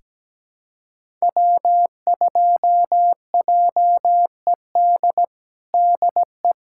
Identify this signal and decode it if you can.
Morse code, heard as a keyed tone, about 17 wpm, 700 Hz. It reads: W2JED DE
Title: USA Callsign DE Callsign - 5 Characters - 17wpm